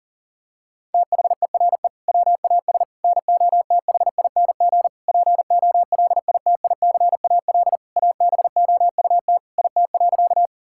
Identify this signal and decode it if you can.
Morse code, heard as a keyed tone, about 40 wpm, 700 Hz. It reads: THERE WAS NOTHING POLITICAL ABOUT IT.
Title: Sentences from Top 1000 Words - Review Part 5 - 40wpm